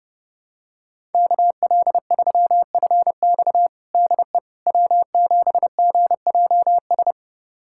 Morse code, heard as a keyed tone, about 30 wpm, 700 Hz. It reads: KL3FX DE W7GJS
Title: USA Callsign DE Callsign - 5 Characters - 30wpm